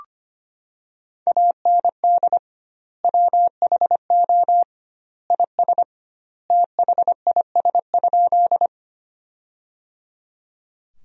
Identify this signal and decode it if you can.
Morse code, heard as a keyed tone, about 25 wpm, 700 Hz. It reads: AND WHO IS THIS?
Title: 2x Word-Spacing - Sentences from Top 100 Words - 25wpm